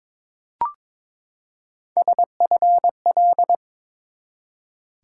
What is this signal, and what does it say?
Morse code, heard as a keyed tone, about 21 wpm, 700 Hz. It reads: SFL
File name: ARRL Sections v2 - 22wpm